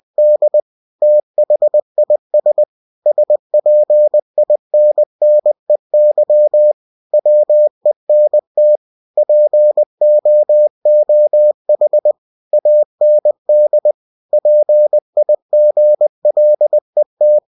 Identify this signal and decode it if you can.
Morse code, heard as a keyed tone, about 20 wpm, 600 Hz. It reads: D THIS SPINNEY WENT POOH AND PIGLET
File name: cw-books / winnie-the-pooh / winnie-20wpm-page-0022